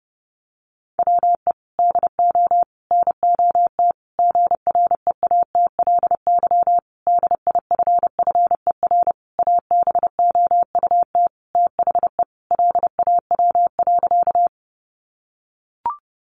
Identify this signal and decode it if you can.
Morse code, heard as a keyed tone, about 30 wpm, 700 Hz. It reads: WE DO NOT GREATLY DIFFER ABOUT THE LAW.
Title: Sentences from Top 1400 Words 30wpm